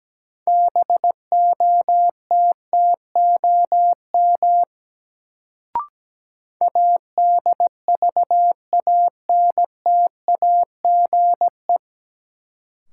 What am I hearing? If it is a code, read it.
Morse code, heard as a keyed tone, about 17 wpm, 700 Hz. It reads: BOTTOM ADVANTAGE
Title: Sentences from Top 900 Words - New Word Drill - 17wpm